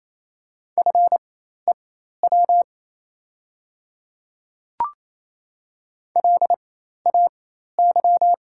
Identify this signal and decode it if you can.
Morse code, heard as a keyed tone, about 28 wpm, spaced as for 13 wpm, 700 Hz. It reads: FEW LAY
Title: 4x Character Spacing - Top 500 Words - 28wpm